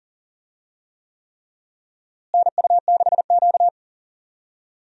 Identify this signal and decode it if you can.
Morse code, heard as a keyed tone, about 40 wpm, 700 Hz. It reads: NU6Q
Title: USA Call Signs - 4 Characters - Repeated 3x - 40wpm